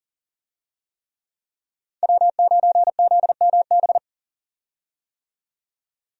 Morse code, heard as a keyed tone, about 40 wpm, 700 Hz. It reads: W9ZMB
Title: USA Call Signs - 5 Characters - 40wpm